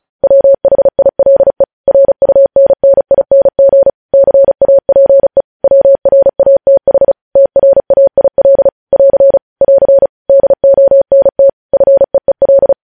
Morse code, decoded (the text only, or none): WHILE RUNNING CAPE WRATH TRAIL <AR> <AR> DONT FEEL